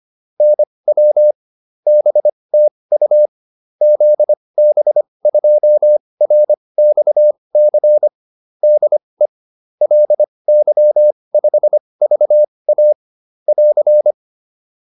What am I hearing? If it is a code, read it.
Morse code, heard as a keyed tone, about 25 wpm, spaced as for 20 wpm, 600 Hz. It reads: NW BTU ZB2RXC DE LY5VA <AR>
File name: QSO406 20